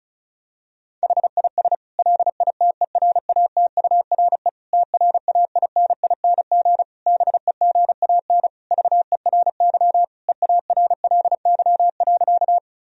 Morse code, heard as a keyed tone, about 36 wpm, 700 Hz. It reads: HIS LITERATURE TRAINING BEGAN VERY EARLY.
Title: Sentences from Top 1000 Words - Review Part 4 - 35wpm